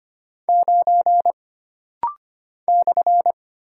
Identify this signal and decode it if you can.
Morse code, heard as a keyed tone, about 25 wpm, 700 Hz. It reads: 9 /
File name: Single Character (Letter:Number:Punctuation:Prosign) - Rapid-Fire - 25wpm